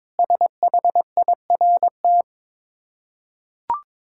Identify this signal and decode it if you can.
Morse code, heard as a keyed tone, about 22 wpm, 700 Hz. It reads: SHIRT
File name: practice / 22wpm / Difficult Words with Hints - 22wpm